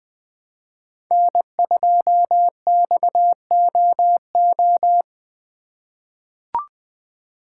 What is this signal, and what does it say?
Morse code, heard as a keyed tone, about 20 wpm, 700 Hz. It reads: N2XOO
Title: USA Call Signs - 5 Characters - 2x Character Spacing - 20wpm